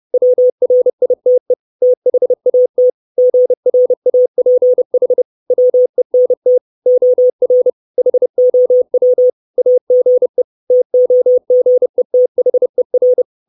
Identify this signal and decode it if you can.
Morse code, heard as a keyed tone, about 30 wpm, 500 Hz. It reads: WRITE THAT GRAPH WENT OR HOW AGE TOGETHER
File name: 632 Words 30WPM